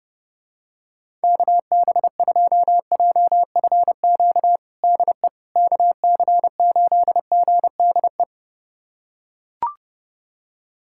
Morse code, heard as a keyed tone, about 30 wpm, 700 Hz. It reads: KB2JFQ DE KC8GDE
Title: USA Callsign DE Callsign - 6 Characters - 30wpm